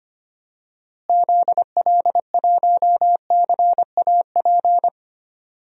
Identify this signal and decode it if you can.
Morse code, heard as a keyed tone, about 25 wpm, 700 Hz. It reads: ZL1CAP
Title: DXCC-100 Call Signs - 6 Characters - Repeated 2x - 25wpm